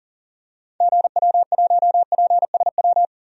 Morse code, heard as a keyed tone, about 40 wpm, 700 Hz. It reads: GW1PSW
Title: DXCC-100 Call Signs - 6 Characters - 40wpm